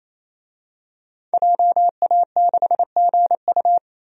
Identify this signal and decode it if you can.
Morse code, heard as a keyed tone, about 28 wpm, 700 Hz. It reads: JA6GU
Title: DXCC-100 Call Signs - 5 Characters - 28wpm